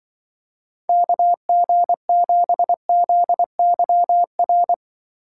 Morse code, heard as a keyed tone, about 23 wpm, 700 Hz. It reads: KG7ZYR
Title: Speed Racing - USA 2x3 Call Signs 20wpm